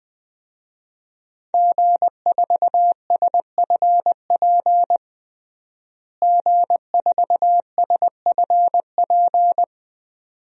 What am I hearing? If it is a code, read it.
Morse code, heard as a keyed tone, about 20 wpm, 700 Hz. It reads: G4SFP G4SFP
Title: DXCC-100 Call Signs - 5 Characters - Repeated 2x - 20wpm